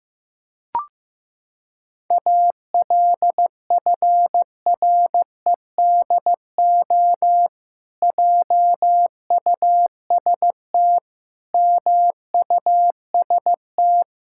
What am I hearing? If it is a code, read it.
Morse code, heard as a keyed tone, about 15 wpm, 700 Hz. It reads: ALFREDO JUST MUST
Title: Pangram Challenge v2 - 15wpm